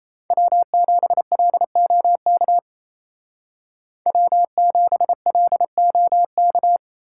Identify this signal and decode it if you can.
Morse code, heard as a keyed tone, about 33 wpm, 700 Hz. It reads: W7LOK W7LOK
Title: Speed Racing - USA 1x3 Call Signs 28wpm